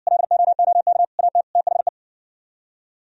Morse code, heard as a keyed tone, about 59 wpm, 700 Hz. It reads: LOOK AT THE